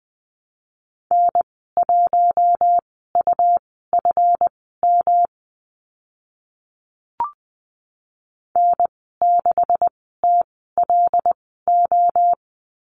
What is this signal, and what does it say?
Morse code, heard as a keyed tone, about 20 wpm, spaced as for 14 wpm, 700 Hz. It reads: N1UFM N6TLO
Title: 173 Lesson 28g - Number 9 - Call Signs 20wpm